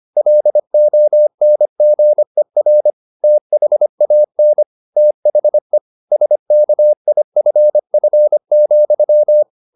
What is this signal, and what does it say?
Morse code, heard as a keyed tone, about 25 wpm, 600 Hz. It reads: LONGER THAN THE SKIFF,
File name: the old man and the sea part 1 25wpm